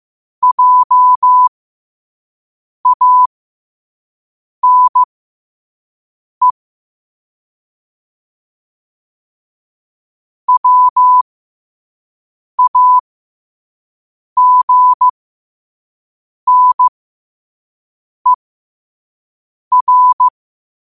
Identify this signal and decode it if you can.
Morse code, heard as a keyed tone, about 15 wpm, spaced as for 5 wpm, 1000 Hz. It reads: JANE WAGNER
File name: qotdfiles / 5wpm